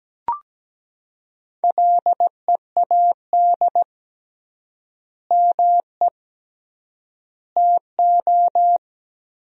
Morse code, heard as a keyed tone, about 17 wpm, 700 Hz. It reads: LEAD ME TO
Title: 3x Word-Spacing - Sentences from Top 500 Words - 17wpm